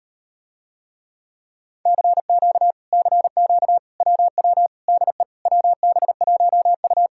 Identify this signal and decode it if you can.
Morse code, heard as a keyed tone, about 38 wpm, 700 Hz. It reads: CQ CQ WW DE WB1U